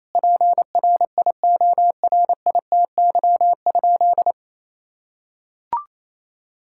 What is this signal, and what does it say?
Morse code, heard as a keyed tone, about 28 wpm, 700 Hz. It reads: PRIORITY?
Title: Sentences from Top 1600 Words 28wpm